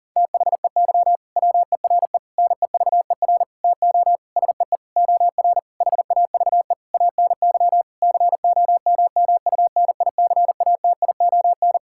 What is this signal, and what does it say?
Morse code, heard as a keyed tone, about 40 wpm, 700 Hz. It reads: THEY WERE NEVER TO SEE OR HAVE ANY COMMUNICATION
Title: Sentences from Top 1000 Words - Review Part 6 - 40wpm